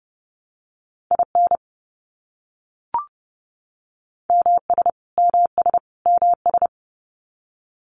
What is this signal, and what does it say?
Morse code, heard as a keyed tone, about 30 wpm, 700 Hz. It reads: IN MS MS MS
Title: US States Repeated 3x 30wpm